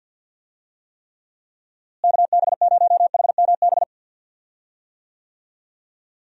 Morse code, heard as a keyed tone, about 50 wpm, 700 Hz. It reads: KB0HMB
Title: USA Call Signs - 6 Characters - Repeated 2x - 50wpm